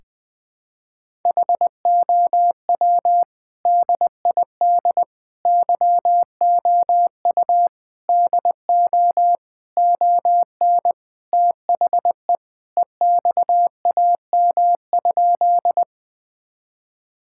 Morse code, heard as a keyed tone, about 21 wpm, 700 Hz. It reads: HOW DID YOU DO ON THE EXAM?
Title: Sentences from Top 1000 Words - Review Part 5 - 20wpm